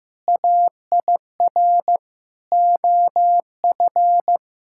Morse code, heard as a keyed tone, about 15 wpm, 700 Hz. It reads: AIR OF
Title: Sentences from Top 1600 Words 15wpm